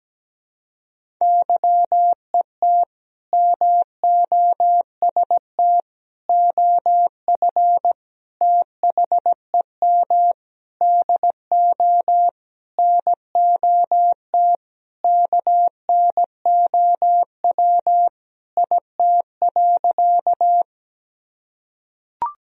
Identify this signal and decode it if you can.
Morse code, heard as a keyed tone, about 17 wpm, 700 Hz. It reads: YET MOST OF THEM DO NOT KNOW IT.
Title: Sentences from Top 1000 Words - Review Part 1 - 17wpm